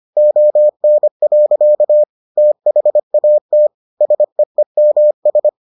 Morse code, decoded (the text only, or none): ON. THAT SEEMS